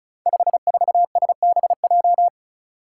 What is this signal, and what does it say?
Morse code, heard as a keyed tone, about 36 wpm, 700 Hz. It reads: 54SBJ
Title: Five Challenging Letter-Number Combinations 35wpm